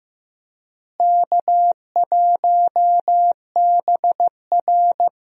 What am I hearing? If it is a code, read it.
Morse code, heard as a keyed tone, about 15 wpm, 700 Hz. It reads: K1BR
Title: USA Call Signs - 4 Characters - Repeated 3x - 15wpm